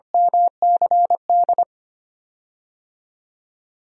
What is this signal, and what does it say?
Morse code, heard as a keyed tone, about 25 wpm, 700 Hz. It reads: MCD
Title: DXCC-100 Call Signs - 6 Characters - Repeated 2x - 25wpm